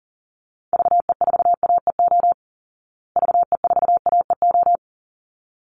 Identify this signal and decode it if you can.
Morse code, heard as a keyed tone, about 40 wpm, 700 Hz. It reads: VE4AEO VE4AEO
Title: Call Signs Repeated 3x - 40wpm